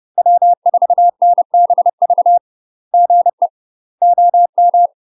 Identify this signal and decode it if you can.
Morse code, heard as a keyed tone, about 30 wpm, 700 Hz. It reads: W4NBV GE OM
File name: UK MASTER QSOs TRAINING / 30WPM / UK MASTER QSOs 0035